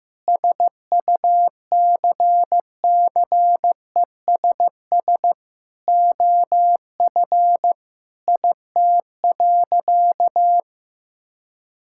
Morse code, heard as a keyed tone, about 15 wpm, 700 Hz. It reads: SUCCESS OF IT.
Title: Sentences from Top 1000 Words - Review Part 6 - 15wpm